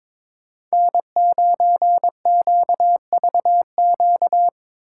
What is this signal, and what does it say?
Morse code, heard as a keyed tone, about 21 wpm, 700 Hz. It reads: N9QVQ
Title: USA Callsign DE Callsign - 5 Characters - 22wpm